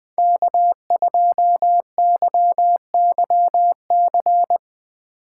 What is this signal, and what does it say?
Morse code, heard as a keyed tone, about 21 wpm, 700 Hz. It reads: K2YYC
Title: Speed Racing - USA 1x3 Call Signs 17wpm